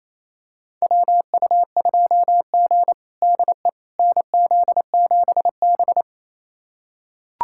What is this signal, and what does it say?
Morse code, heard as a keyed tone, about 28 wpm, 700 Hz. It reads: WU2G DE NZ7B
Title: USA Callsign DE Callsign - 4 Characters - 28wpm